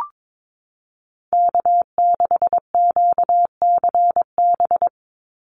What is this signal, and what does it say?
Morse code, heard as a keyed tone, about 22 wpm, 700 Hz. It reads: K6QCB
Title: NA SOTA Chaser Callsigns 22wpm